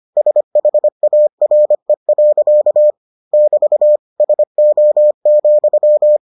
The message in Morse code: SHARE. <BT> SO,

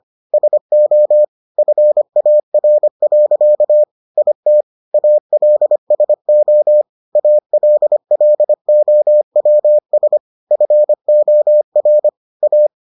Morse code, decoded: SO FAR. IT ALSO ALLOWS FOR A

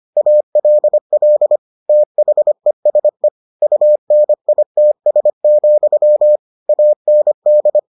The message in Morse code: ALL THESE UNITS, AND

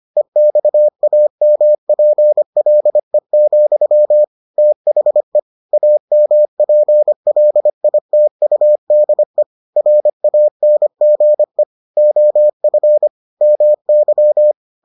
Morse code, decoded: EXAMPLE, THE AMPLITUDE RANGE OF MY

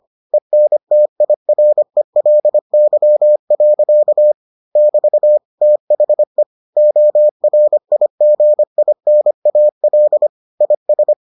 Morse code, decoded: ENTIRELY. <BT> THE ORIGINAL IS

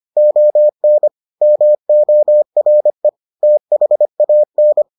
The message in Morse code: ON MORE THAN

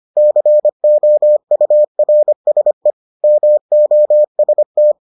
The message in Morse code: COURSE MOST

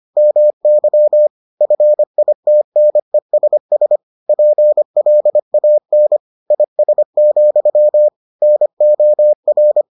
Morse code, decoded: MY FITNESS PLAN IS, NOR

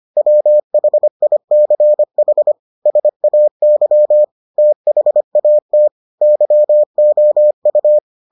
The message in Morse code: WHICH SAY THAT YOU

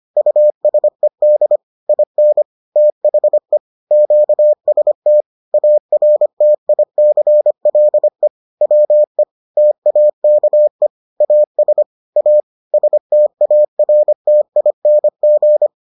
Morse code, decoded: USED IN THE QST ARTICLE WE TAKE AS A STARTING

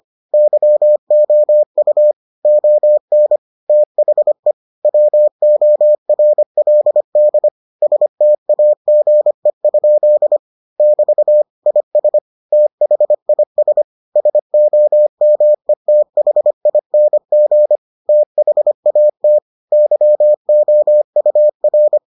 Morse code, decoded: YOU ON THE WORLD STAGE? <BT> IS THIS SOMETHING THAT YOUR

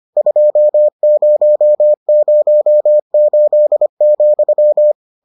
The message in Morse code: 2008,

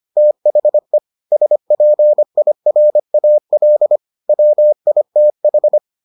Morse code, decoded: THE SPIRAL WITH

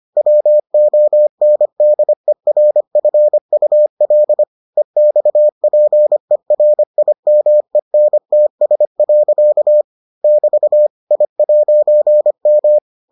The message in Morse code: WONDERFUL EXPERIMENTS. <BT> I'M

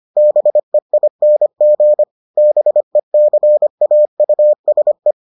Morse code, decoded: BEING BECAUSE

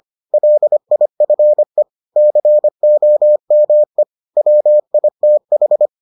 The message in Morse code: LIFE COME WITH

